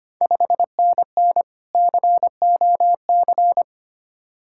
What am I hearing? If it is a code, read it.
Morse code, heard as a keyed tone, about 25 wpm, 700 Hz. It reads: 5NN COC
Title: Ontario QSO Party Exchange 25wpm